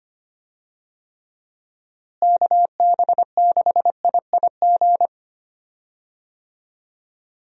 Morse code, heard as a keyed tone, about 25 wpm, 700 Hz. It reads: KB6IIG
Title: USA Call Signs - 6 Characters - Repeated 2x - 25wpm